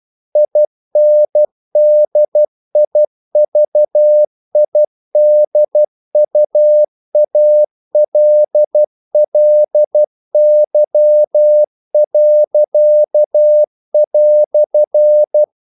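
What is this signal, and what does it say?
Morse code, heard as a keyed tone, about 12 wpm, 600 Hz. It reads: INDIVIDUALLY."
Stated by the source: Morse practice sending English quotations.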